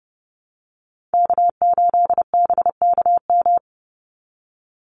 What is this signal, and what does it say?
Morse code, heard as a keyed tone, about 30 wpm, 700 Hz. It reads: K8BKM
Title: Call Signs Repeated 2x - 30wpm